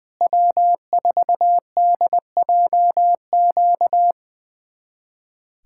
Morse code, heard as a keyed tone, about 20 wpm, 700 Hz. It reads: W4DJQ
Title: USA Callsign DE Callsign - 5 Characters - 20wpm